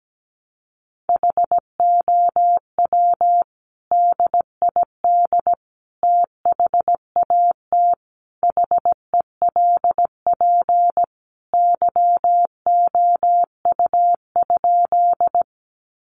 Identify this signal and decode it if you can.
Morse code, heard as a keyed tone, about 17 wpm, 700 Hz. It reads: HOW DID THAT HELP YOU?
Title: Sentences from Top 200 Words 17wpm